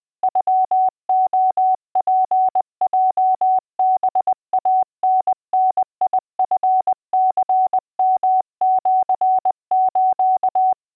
Morse code (Japanese Wabun code), heard as a keyed tone, about 20 wpm, 750 Hz. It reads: ノレツヲハイタダチニヨシス